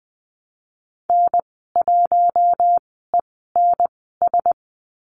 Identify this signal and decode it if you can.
Morse code, heard as a keyed tone, about 20 wpm, spaced as for 15 wpm, 700 Hz. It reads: N1ENS
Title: 128 Lesson 21g - Number 6 - Call Signs 20wpm